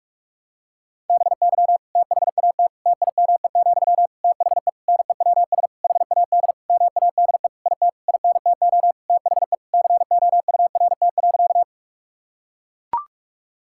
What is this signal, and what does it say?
Morse code, heard as a keyed tone, about 41 wpm, 700 Hz. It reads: BY THAT TIME, THE NEWS HAD MADE IT INTO THE COURT.
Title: Sentences from Top 1600 Words 45wpm